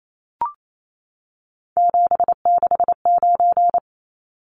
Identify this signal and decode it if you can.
Morse code, heard as a keyed tone, about 28 wpm, 700 Hz. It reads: 769
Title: Numbers - 3-Digit 28wpm